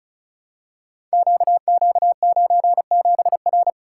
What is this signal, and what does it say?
Morse code, heard as a keyed tone, about 36 wpm, 700 Hz. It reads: QQ97R
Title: Code Groups - 5 Letter:Numbers - 35wpm